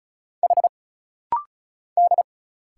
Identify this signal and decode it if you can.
Morse code, heard as a keyed tone, about 34 wpm, 700 Hz. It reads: H D
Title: Single Letters - Rapid-Fire 35wpm